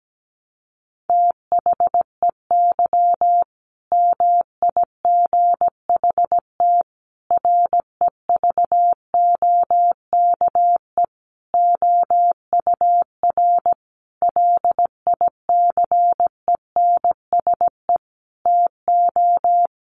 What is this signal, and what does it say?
Morse code, heard as a keyed tone, about 17 wpm, 700 Hz. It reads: THEY MIGHT REVOKE OUR LICENSE TO
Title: A Sound Of Thunder 17wpm